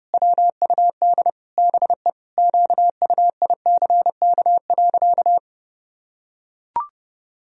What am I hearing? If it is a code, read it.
Morse code, heard as a keyed tone, about 32 wpm, 700 Hz. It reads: WUD BE QUICK.
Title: Sentences from Top 500 Words and Abbreviations 30wpm